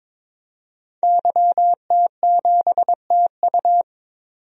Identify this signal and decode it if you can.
Morse code, heard as a keyed tone, about 22 wpm, 700 Hz. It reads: YT7TU
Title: DXCC-100 Call Signs - 5 Characters - 22wpm